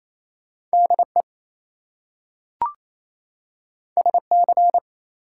Morse code, heard as a keyed tone, about 28 wpm, 700 Hz. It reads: DE SC
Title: US States and Canadian Provinces 28wpm